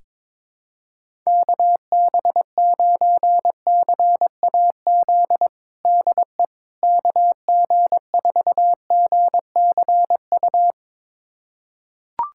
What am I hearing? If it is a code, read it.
Morse code, heard as a keyed tone, about 22 wpm, 700 Hz. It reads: KB9CAZ DE KG4GCU
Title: USA Callsign DE Callsign 22wpm